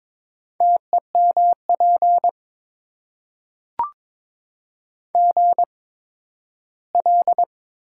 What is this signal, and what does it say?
Morse code, heard as a keyed tone, about 22 wpm, spaced as for 13 wpm, 700 Hz. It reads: TEMP GL
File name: Top 90 QSO Elements - 8x Character Spacing - 22wpm